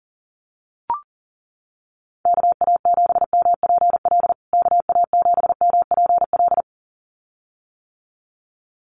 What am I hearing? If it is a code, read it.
Morse code, heard as a keyed tone, about 37 wpm, 700 Hz. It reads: KA7MPL KA7MPL
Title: USA Call Signs Repeated 2x 40wpm